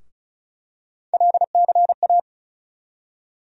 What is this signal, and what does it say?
Morse code, heard as a keyed tone, about 36 wpm, 700 Hz. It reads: LCA
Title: USA Call Sign Postfixes - 3 Letters 35wpm